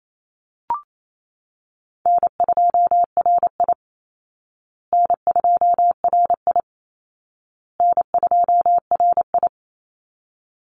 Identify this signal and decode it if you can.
Morse code, heard as a keyed tone, about 28 wpm, 700 Hz. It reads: N2RI N2RI N2RI
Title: Call Signs Repeated 2x - 28wpm